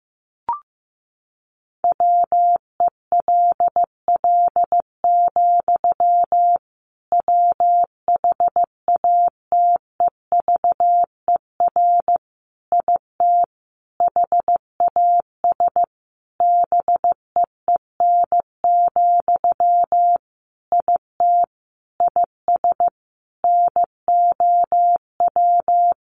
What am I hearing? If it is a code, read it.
Morse code, heard as a keyed tone, about 15 wpm, 700 Hz. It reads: WELL, WHATEVER IT HAS BEEN, IT IS NOW
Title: Sentences from Top 1100 Words 15wpm